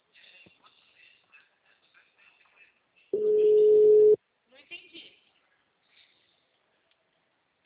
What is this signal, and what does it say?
Ringback tone (Europe, 425 Hz), with speech.